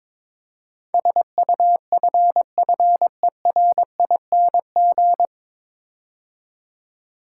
Morse code, heard as a keyed tone, about 22 wpm, 700 Hz. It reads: SUFFERING
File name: 9 Letter Words - First 3 Letters Given - 22wpm